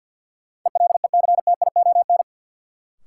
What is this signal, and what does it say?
Morse code, heard as a keyed tone, about 46 wpm, 700 Hz. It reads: ELECTION
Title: Sentences from Top 1500 Words - New Word Drill - 50wpm